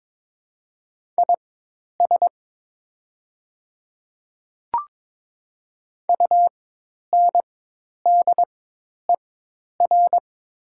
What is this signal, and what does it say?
Morse code, heard as a keyed tone, about 23 wpm, spaced as for 10 wpm, 700 Hz. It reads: IS UNDER
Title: 4x Character Spacing - Top 200 Words - 22wpm